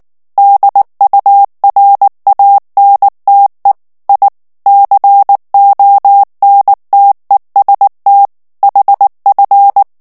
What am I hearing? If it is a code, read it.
Morse code, heard as a keyed tone, about 19 wpm, 800 Hz. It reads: DURANTE I CONTEST HF
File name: morse mp3 / ITALIEN texte clair 2 vitesse 18 mots minute